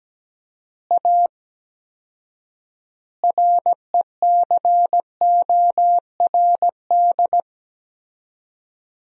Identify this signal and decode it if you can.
Morse code, heard as a keyed tone, about 17 wpm, 700 Hz. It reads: A RECORD
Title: 4x Word-Spacing - Sentences from Top 500 Words - 17wpm